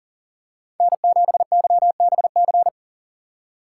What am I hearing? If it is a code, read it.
Morse code, heard as a keyed tone, about 40 wpm, 700 Hz. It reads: N7YBC
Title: USA Call Signs - 6x Character Spacing - 40wpm